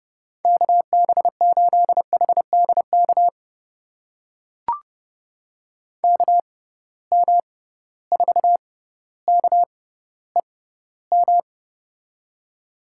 Morse code, heard as a keyed tone, about 32 wpm, spaced as for 16 wpm, 700 Hz. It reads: KB8HDK KM4KEM
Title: USA Call Signs - 6 Characters - 6x Character Spacing - 30wpm